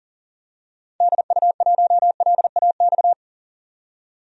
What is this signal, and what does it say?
Morse code, heard as a keyed tone, about 40 wpm, 700 Hz. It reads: DU1LAX
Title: DXCC-100 Call Signs - 6 Characters - 40wpm